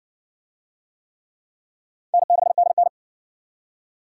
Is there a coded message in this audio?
Yes: Morse code from 2.1 to 2.9 s, heard as a keyed tone, about 54 wpm, 700 Hz. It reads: N6DN